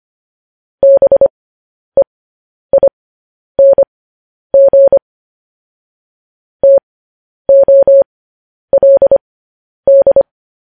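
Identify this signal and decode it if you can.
Morse code, heard as a keyed tone, about 25 wpm, spaced as for 10 wpm, 550 Hz. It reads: BEING TOLD
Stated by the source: Morse code news headlines broadcast.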